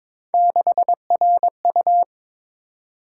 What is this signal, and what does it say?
Morse code, heard as a keyed tone, about 21 wpm, 700 Hz. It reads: 6RU
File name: practice / 22wpm / Three Letter-Number Combinations - 22wpm